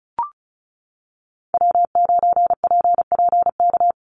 Morse code, heard as a keyed tone, about 34 wpm, 700 Hz. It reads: W9PPK
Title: USA Call Signs 35wpm